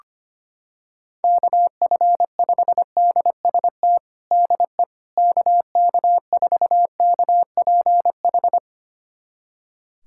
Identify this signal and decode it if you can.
Morse code, heard as a keyed tone, about 25 wpm, 700 Hz. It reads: KF5DST DE KK4KPH
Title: USA Callsign DE Callsign 25wpm